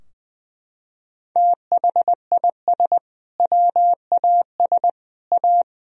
Morse code, heard as a keyed tone, about 21 wpm, 700 Hz. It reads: THIS WAS A
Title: Sentences from Top 1000 Words - Review Part 2 - 20wpm